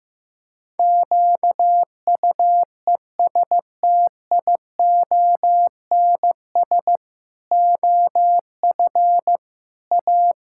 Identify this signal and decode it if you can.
Morse code, heard as a keyed tone, about 15 wpm, 700 Hz. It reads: QUESTIONS OF A
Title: Pangram Challenge v2 - 15wpm